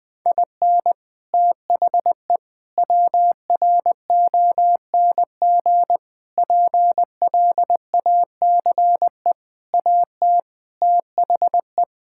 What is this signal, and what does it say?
Morse code, heard as a keyed tone, about 20 wpm, 700 Hz. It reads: IN THE WRONG PLACE AT THE
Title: Sets of 8 Words - 20wpm